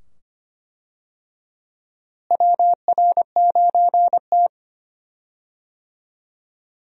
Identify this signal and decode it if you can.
Morse code, heard as a keyed tone, about 25 wpm, 700 Hz. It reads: WR9T
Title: USA Call Signs - 4 Characters - Repeated 2x - 25wpm